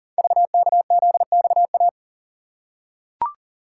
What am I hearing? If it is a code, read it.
Morse code, heard as a keyed tone, about 41 wpm, 690 Hz. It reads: VKZXA
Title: Code Groups - 5 Letters - 40wpm